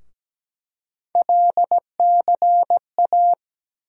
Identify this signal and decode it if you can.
Morse code, heard as a keyed tone, about 17 wpm, 700 Hz. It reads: LCA
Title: USA Call Sign Postfixes - 3 Letters 17wpm